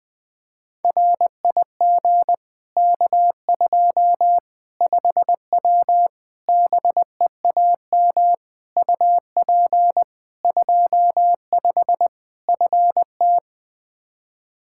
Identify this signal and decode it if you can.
Morse code, heard as a keyed tone, about 21 wpm, 700 Hz. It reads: RIG K2 5W BEAM UP 25 FT
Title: Practice QSO - Exchange 2 - Info - 20wpm